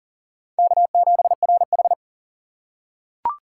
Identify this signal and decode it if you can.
Morse code, heard as a keyed tone, about 40 wpm, 700 Hz. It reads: K7RH
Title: USA Call Signs - 4 Characters - 4x Character Spacing - 40wpm